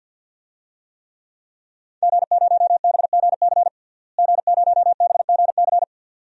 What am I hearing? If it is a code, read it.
Morse code, heard as a keyed tone, about 51 wpm, 700 Hz. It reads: G0BGC G0BGC
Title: DXCC-100 Call Signs - 5 Characters - Repeated 2x - 50wpm